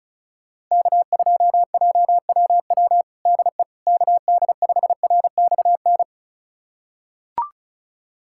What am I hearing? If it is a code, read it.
Morse code, heard as a keyed tone, about 36 wpm, 700 Hz. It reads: K2JWW DE KD5RXN
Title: USA Callsign DE Callsign 35wpm